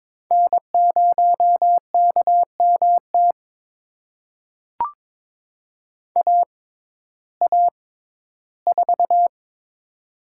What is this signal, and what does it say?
Morse code, heard as a keyed tone, about 22 wpm, spaced as for 14 wpm, 700 Hz. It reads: N0KMT AA4
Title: USA Call Signs - 5 Characters - 6x Character Spacing - 22wpm